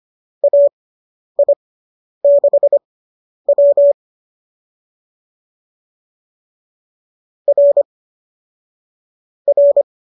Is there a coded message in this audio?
Yes: Morse code from 0.4 to 9.8 s, heard as a keyed tone, about 25 wpm, spaced as for 10 wpm, 570 Hz. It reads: AI6W R R